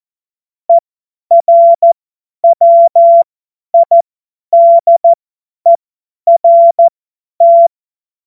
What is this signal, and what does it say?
Morse code, heard as a keyed tone, about 14 wpm, spaced as for 10 wpm, 680 Hz. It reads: ERWIDERT
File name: Morsekurs / Witze / Witz 11. Juli 2023 50 F70